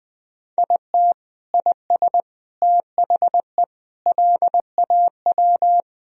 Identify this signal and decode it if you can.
Morse code, heard as a keyed tone, about 20 wpm, 700 Hz. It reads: IT IS THE LAW
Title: Sentences from Top 1000 Words - Review Part 3 - 20wpm